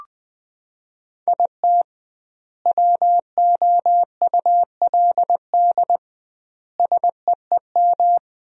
Morse code, heard as a keyed tone, about 20 wpm, 700 Hz. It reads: IT WOULD SEEM
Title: 2x Word-Spacing - Sentences from Top 300 Words - 20wpm